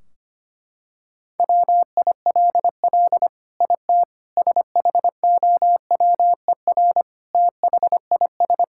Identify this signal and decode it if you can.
Morse code, heard as a keyed tone, about 25 wpm, 700 Hz. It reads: WILL IT SHOWER THIS
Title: Sentences from Top 1600 Words 25wpm